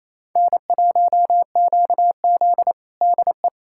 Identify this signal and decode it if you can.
Morse code, heard as a keyed tone, about 28 wpm, 700 Hz. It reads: N1QZ DE
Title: USA Callsign DE Callsign - 4 Characters - 28wpm